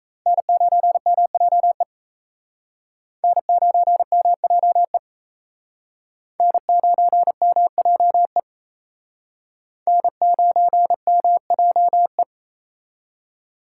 Morse code, heard as a keyed tone, about 34 wpm, 700 Hz. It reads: N9MJE N9MJE N9MJE N9MJE
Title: Speed Racing - USA 1x3 Call Signs 28wpm